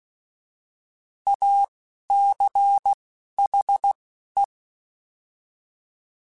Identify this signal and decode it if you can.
Morse code, heard as a keyed tone, about 16 wpm, spaced as for 12 wpm, 800 Hz. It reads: ACHE